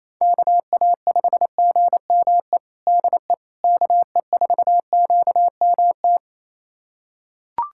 Morse code, heard as a keyed tone, about 28 wpm, 700 Hz. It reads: KA5GME DE KE4QMT
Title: USA Callsign DE Callsign - 6 Characters - 28wpm